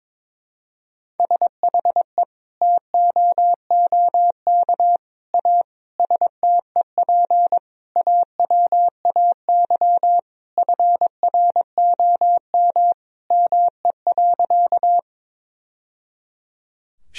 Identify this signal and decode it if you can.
Morse code, heard as a keyed tone, about 22 wpm, 700 Hz. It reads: SHE TOOK A STEP AWAY FROM ME.
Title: Sentences from Top 1000 Words - Review Part 2 - 22wpm